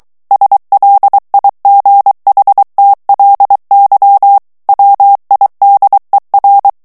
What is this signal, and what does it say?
Morse code, heard as a keyed tone, about 23 wpm, 800 Hz. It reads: SLIGHTLY WIDER
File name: ANGLAIS texte clair 3 vitesse 22 mots minute